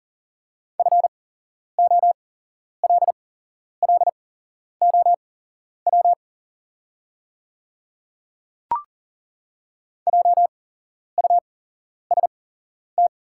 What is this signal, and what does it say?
Morse code, heard as a keyed tone, about 40 wpm, spaced as for 11 wpm, 700 Hz. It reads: FOLLOW JUST
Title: 8x Character Spacing - Top 300 Words - 40wpm